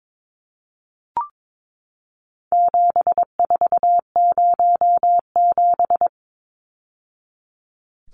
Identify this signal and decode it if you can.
Morse code, heard as a keyed tone, about 22 wpm, 700 Hz. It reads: 7407
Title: Numbers - 4-Digit 22wpm